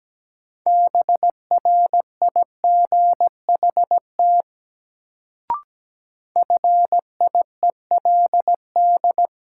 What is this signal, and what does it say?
Morse code, heard as a keyed tone, about 17 wpm, 700 Hz. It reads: BRIGHT FIELD
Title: Sentences from Top 1000 Words - New Word Drill - 17wpm